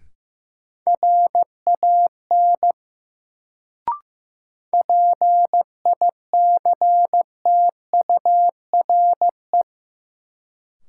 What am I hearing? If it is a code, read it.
Morse code, heard as a keyed tone, about 15 wpm, 700 Hz. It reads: RAN PICTURE
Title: Sentences from Top 500 Words - New Word Drill - 15wpm